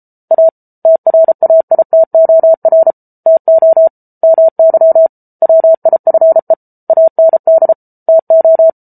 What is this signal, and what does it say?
Morse code, heard as a keyed tone, about 34 wpm, 650 Hz. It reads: A TRAITOR TO MY WIFE AND TO